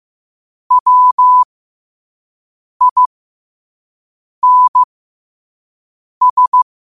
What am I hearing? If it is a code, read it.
Morse code, heard as a keyed tone, about 15 wpm, spaced as for 5 wpm, 1000 Hz. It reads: WINS
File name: qotdfiles / 5wpm